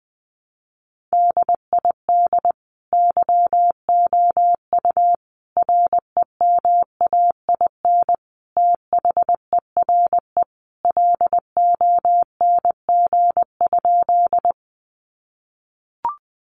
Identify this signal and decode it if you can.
Morse code, heard as a keyed tone, about 20 wpm, 700 Hz. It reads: DID YOU REMAIN THERE LONG?
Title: Sentences from Top 1200 Words 20wpm